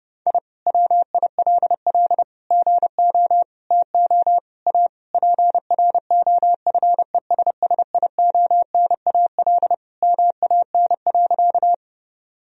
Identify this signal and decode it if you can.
Morse code, heard as a keyed tone, about 30 wpm, 700 Hz. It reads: I WILL GO TO A PROFESSIONAL MAN.
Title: Sentences from Top 1000 Words - Review Part 4 - 30wpm